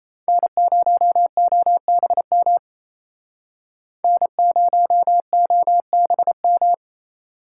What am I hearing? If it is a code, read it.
Morse code, heard as a keyed tone, about 35 wpm, 700 Hz. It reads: N0OBM N0OBM